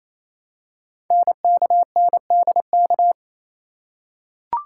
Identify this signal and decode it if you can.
Morse code, heard as a keyed tone, about 27 wpm, 700 Hz. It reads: NKNDK
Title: Code Groups - 5 Letters - 28wpm